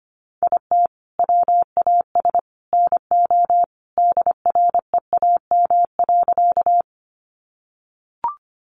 Morse code, heard as a keyed tone, about 25 wpm, 700 Hz. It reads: IT WAS NO DREAM.
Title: Sentences from Top 1500 Words 25wpm